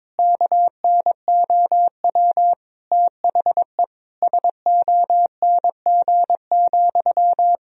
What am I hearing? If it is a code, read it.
Morse code, heard as a keyed tone, about 21 wpm, 700 Hz. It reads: KNOW THE SONG,
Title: Sentences from Top 1000 Words - Review Part 2 - 22wpm